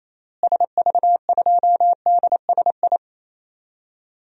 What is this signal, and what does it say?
Morse code, heard as a keyed tone, about 28 wpm, 700 Hz. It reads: SV2DSI
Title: DXCC-100 Call Signs - 6 Characters - Repeated 3x - 28wpm